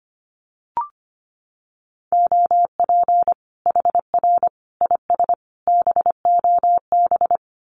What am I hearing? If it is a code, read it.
Morse code, heard as a keyed tone, about 25 wpm, 700 Hz. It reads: OP HR IS BOB
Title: Common QSO Phrases 25wpm